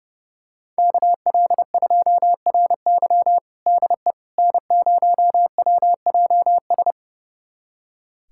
Morse code, heard as a keyed tone, about 30 wpm, 700 Hz. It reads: KL2RY DE N0WJS
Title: USA Callsign DE Callsign - 5 Characters - 30wpm